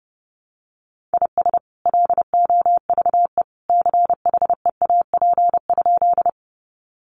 Morse code, heard as a keyed tone, about 30 wpm, 700 Hz. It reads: IS LOVE CHEAP?
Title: Sentences from Top 1200 Words 30wpm